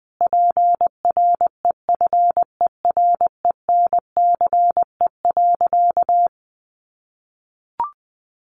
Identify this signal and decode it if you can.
Morse code, heard as a keyed tone, about 20 wpm, 700 Hz. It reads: PREFERENCE.
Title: Sentences from Top 1500 Words 20wpm